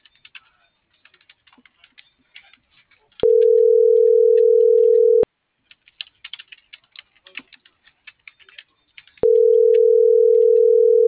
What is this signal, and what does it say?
Ringback tone (North America), 440 + 480 Hz.
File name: CENTRAL MORENA / out ruth.skyangel.24